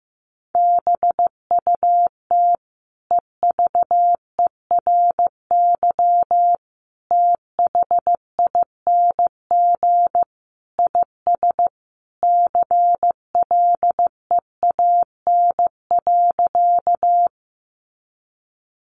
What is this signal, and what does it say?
Morse code, heard as a keyed tone, about 15 wpm, 700 Hz. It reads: BUT EVERYTHING IS CLEAN.
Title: Sentences from Top 1100 Words 15wpm